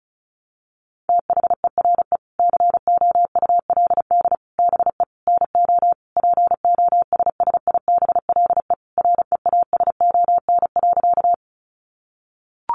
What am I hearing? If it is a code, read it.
Morse code, heard as a keyed tone, about 34 wpm, 700 Hz. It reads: THERE COULD BE NO POSSIBLE REASON.
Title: Sentences from Top 700 Words 35wpm